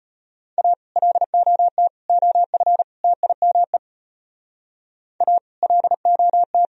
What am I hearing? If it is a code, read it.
Morse code, heard as a keyed tone, about 33 wpm, 700 Hz. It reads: A LOT OF TIME A LOT